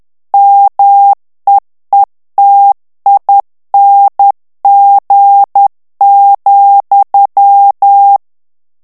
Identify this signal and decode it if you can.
Morse code, heard as a keyed tone, about 10 wpm, 800 Hz. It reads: MEETING,
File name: ANGLAIS texte clair 3 vitesse 10 mots minute